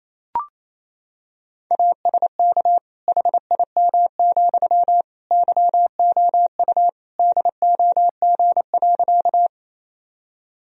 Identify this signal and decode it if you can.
Morse code, heard as a keyed tone, about 28 wpm, 700 Hz. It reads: ASK HIM, YOU DOG.
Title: Sentences from Top 1000 Words - Review Part 2 - 28wpm